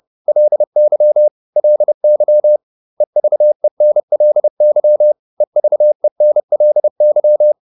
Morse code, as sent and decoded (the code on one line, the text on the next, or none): .-.. -.-- / .-.. -.-- / . ...- . -. .-.. -.-- / . ...- . -. .-.. -.--
LY LY EVENLY EVENLY